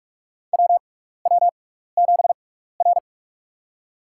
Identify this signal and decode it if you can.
Morse code, heard as a keyed tone, about 42 wpm, spaced as for 15 wpm, 700 Hz. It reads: WW7R
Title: USA Call Signs - 4 Characters - 6x Character Spacing - 45wpm